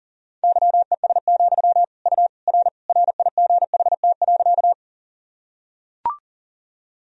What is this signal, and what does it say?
Morse code, heard as a keyed tone, about 40 wpm, 700 Hz. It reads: YES, U R RIGHT.
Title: Sentences from Top 500 Words and Abbreviations 40wpm